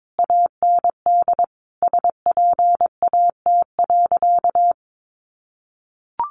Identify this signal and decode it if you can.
Morse code, heard as a keyed tone, about 22 wpm, 700 Hz. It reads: AND SPAT.
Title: A Sound Of Thunder 22wpm